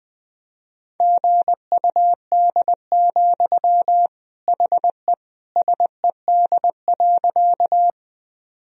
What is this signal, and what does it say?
Morse code, heard as a keyed tone, about 21 wpm, 700 Hz. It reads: GUD, HE SED.
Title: Sentences from Top 200 Words and Abbreviations 20wpm